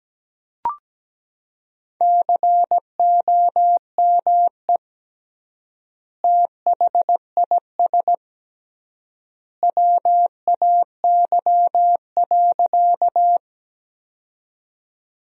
Morse code, heard as a keyed tone, about 17 wpm, 700 Hz. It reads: COME THIS WAY.
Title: 3x Word-Spacing - Sentences from Top 100 Words - 17wpm